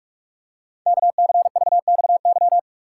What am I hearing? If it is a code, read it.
Morse code, heard as a keyed tone, about 44 wpm, 700 Hz. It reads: KCVXY
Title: Code Groups - 5 Letters - 45wpm